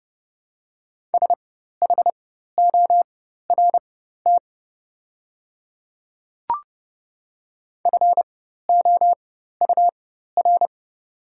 Morse code, heard as a keyed tone, about 29 wpm, spaced as for 14 wpm, 690 Hz. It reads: SHORT FOUR
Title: 4x Character Spacing - Top 500 Words - 30wpm